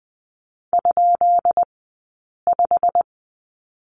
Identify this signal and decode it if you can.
Morse code, heard as a keyed tone, about 20 wpm, spaced as for 8 wpm, 700 Hz. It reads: ?5
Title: 177 Lesson 29d - Slash - Two Characters 20wpm